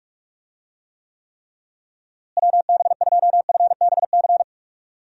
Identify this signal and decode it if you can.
Morse code, heard as a keyed tone, about 41 wpm, 700 Hz. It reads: WB2FBC
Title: USA Call Signs - 6 Characters - Repeated 3x - 45wpm